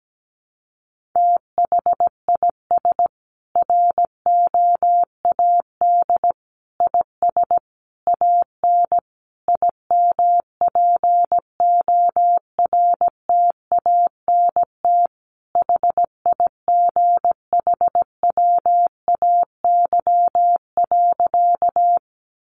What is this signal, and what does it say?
Morse code, heard as a keyed tone, about 17 wpm, 700 Hz. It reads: THIS ROAD IS AN IMPORTANT HIGHWAY.
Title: Sentences from Top 1400 Words 17wpm